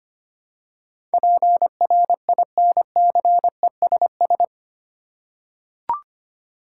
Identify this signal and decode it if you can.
Morse code, heard as a keyed tone, about 25 wpm, 700 Hz. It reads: PRINCESS
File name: 8 Letter Words - First 3 Letters Given - 25wpm